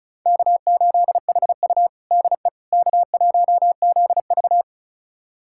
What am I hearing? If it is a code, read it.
Morse code, heard as a keyed tone, about 36 wpm, 700 Hz. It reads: K8HU DE K1ZV
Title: USA Callsign DE Callsign - 4 Characters - 35wpm